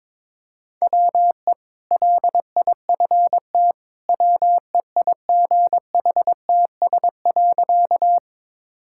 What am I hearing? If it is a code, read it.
Morse code, heard as a keyed tone, about 22 wpm, 700 Hz. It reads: WE LIFT WEIGHTS.
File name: practice / 22wpm / Easy Sentences with Hints - 22wpm